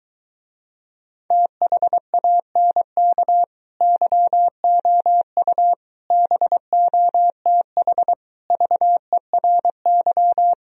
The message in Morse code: THANK YOU BOTH VERY

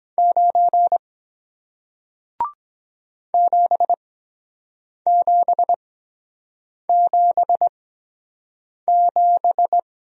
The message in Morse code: 9 7777

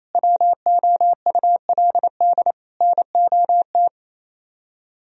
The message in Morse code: WOULD NOT

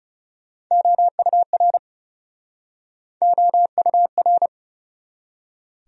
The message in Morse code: OUR OUR

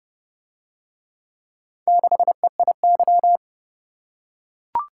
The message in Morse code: 6EIY

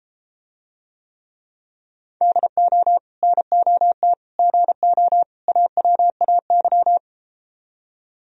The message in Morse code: DO NOT GO AWAY

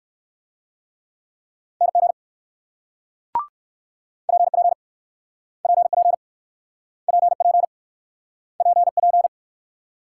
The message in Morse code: NG PP PP PP PP

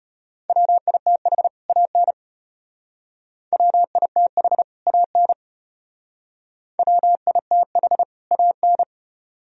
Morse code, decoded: WITH AN WITH AN WITH AN

